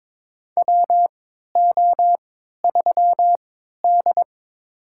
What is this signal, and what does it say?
Morse code, heard as a keyed tone, about 21 wpm, spaced as for 12 wpm, 700 Hz. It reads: WO3D
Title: USA Call Signs - 3x Character Spacing - 22wpm